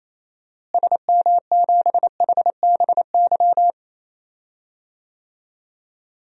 Morse code, heard as a keyed tone, about 28 wpm, 700 Hz. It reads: SM7HBY
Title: DXCC-100 Call Signs - 6 Characters - Repeated 3x - 28wpm